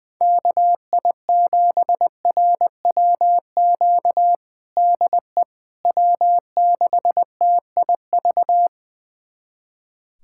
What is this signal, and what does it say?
Morse code, heard as a keyed tone, about 21 wpm, 700 Hz. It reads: KI7RWQ DE W6TIV
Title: USA Callsign DE Callsign 20wpm